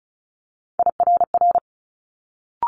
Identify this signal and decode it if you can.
Morse code, heard as a keyed tone, about 36 wpm, 700 Hz. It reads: IRR
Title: Multi-letter Phonemes Quick Fire 35wpm